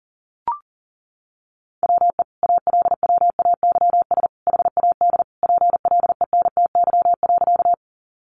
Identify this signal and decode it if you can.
Morse code, heard as a keyed tone, about 40 wpm, 700 Hz. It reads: WE ALWAYS HAD PLENTY.
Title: Sentences from Top 1300 Words 40wpm